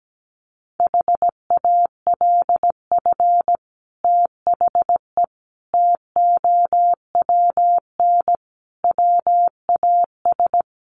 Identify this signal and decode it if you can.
Morse code, heard as a keyed tone, about 17 wpm, 700 Hz. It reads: HALF THE TOWN WAS
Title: Sentences from Top 500 Words 17wpm